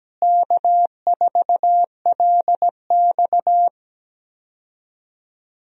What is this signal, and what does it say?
Morse code, heard as a keyed tone, about 17 wpm, 700 Hz. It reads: K4LX
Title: USA Call Signs - 4 Characters - Repeated 3x - 17wpm